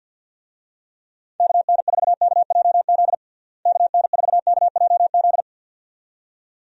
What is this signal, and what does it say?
Morse code, heard as a keyed tone, about 50 wpm, 700 Hz. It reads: KN4KJZ KN4KJZ
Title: USA Call Signs - 6 Characters - Repeated 2x - 50wpm